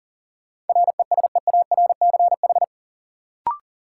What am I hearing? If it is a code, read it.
Morse code, heard as a keyed tone, about 40 wpm, 700 Hz. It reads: RESEARCH
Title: Sentences from Top 1000 Words - Review Part 4 - New Word Drill - 40wpm